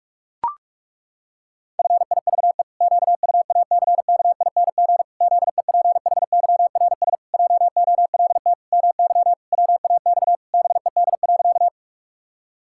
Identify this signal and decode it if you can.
Morse code, heard as a keyed tone, about 41 wpm, 700 Hz. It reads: FIVE QUACKING ZEPHYRS JOLT MY WAX BED.
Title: Pangram Challenge v2 - 45wpm